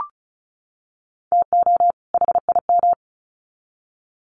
Morse code, heard as a keyed tone, about 34 wpm, 700 Hz. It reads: TO HIM
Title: Sets of 2 Words 35wpm